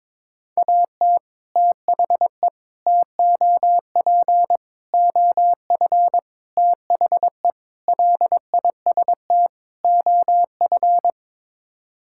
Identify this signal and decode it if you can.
Morse code, heard as a keyed tone, about 22 wpm, 700 Hz. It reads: AT THE TOP OF THE LIST OF
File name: practice / 22wpm / Sets of 7 Words - 22wpm